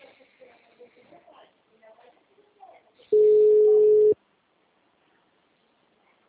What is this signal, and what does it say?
Ringback tone (Europe, 425 Hz), with speech.